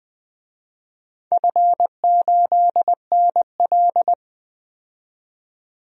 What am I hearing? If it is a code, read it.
Morse code, heard as a keyed tone, about 20 wpm, 700 Hz. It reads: F8NL
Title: DXCC-100 Call Signs - 4 Characters - 20wpm